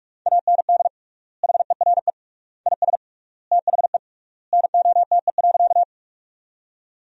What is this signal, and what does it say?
Morse code, heard as a keyed tone, about 45 wpm, 700 Hz. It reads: AND HERE IS THE NOTE.
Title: 3x Word-Spacing - Sentences from Top 500 Words - 45wpm